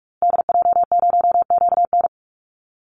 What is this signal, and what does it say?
Morse code, heard as a keyed tone, about 48 wpm, 700 Hz. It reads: DJ0QN
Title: Call Signs Repeated 2x - 45wpm